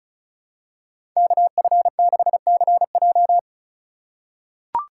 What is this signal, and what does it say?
Morse code, heard as a keyed tone, about 36 wpm, 700 Hz. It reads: KF6CJ
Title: USA Call Signs - 2x Character Spacing - 35wpm